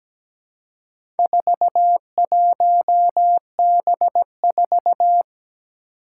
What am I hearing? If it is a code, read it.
Morse code, heard as a keyed tone, about 17 wpm, 700 Hz. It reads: 41B4
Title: Four Challenging Letter-Number Combinations 17wpm